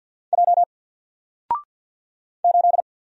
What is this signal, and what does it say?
Morse code, heard as a keyed tone, about 50 wpm, 700 Hz. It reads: J 8
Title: Single Letter-Number 50wpm